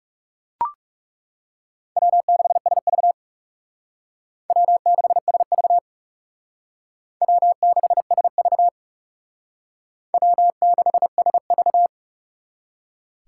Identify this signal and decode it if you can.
Morse code, heard as a keyed tone, about 37 wpm, 700 Hz. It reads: W6SV W6SV W6SV W6SV
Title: Speed Racing - USA 1x2 Call Signs 30wpm